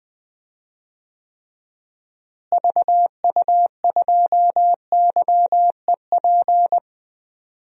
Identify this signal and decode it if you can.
Morse code, heard as a keyed tone, about 20 wpm, 700 Hz. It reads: VU2YEP
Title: DXCC-100 Call Signs - 6 Characters - Repeated 2x - 20wpm